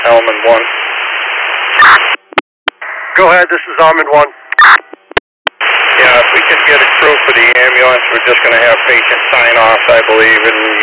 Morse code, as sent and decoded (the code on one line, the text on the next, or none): none